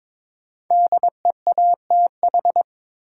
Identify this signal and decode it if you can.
Morse code, heard as a keyed tone, about 22 wpm, 700 Hz. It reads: DEATH
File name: 5 Letter Words - First 3 Letters Given - 22wpm